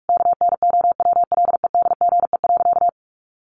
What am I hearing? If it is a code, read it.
Morse code, heard as a keyed tone, about 44 wpm, 700 Hz. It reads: KNOWLEDGE.
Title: Sentences from Top 1200 Words 45wpm